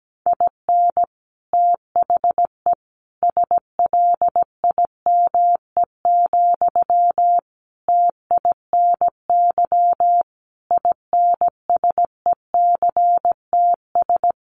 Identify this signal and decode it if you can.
Morse code, heard as a keyed tone, about 17 wpm, 700 Hz. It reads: IN THE SLIME, TINY INSECTS
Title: A Sound Of Thunder 17wpm